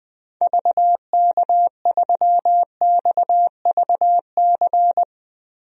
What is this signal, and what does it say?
Morse code, heard as a keyed tone, about 20 wpm, 700 Hz. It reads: VK3XVC
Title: DXCC-100 Call Signs - 6 Characters - Repeated 3x - 20wpm